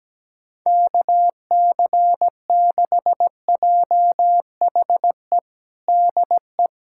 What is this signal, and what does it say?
Morse code, heard as a keyed tone, about 17 wpm, 700 Hz. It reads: KC6JHE DE
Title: USA Callsign DE Callsign 17wpm